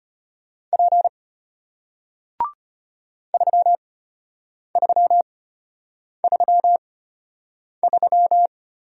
Morse code, heard as a keyed tone, about 33 wpm, spaced as for 8 wpm, 700 Hz. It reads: P 3333